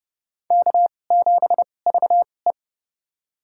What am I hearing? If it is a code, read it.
Morse code, heard as a keyed tone, about 31 wpm, spaced as for 22 wpm, 700 Hz. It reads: K7VE
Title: USA Call Signs - 4 Characters - 2x Character Spacing - 30wpm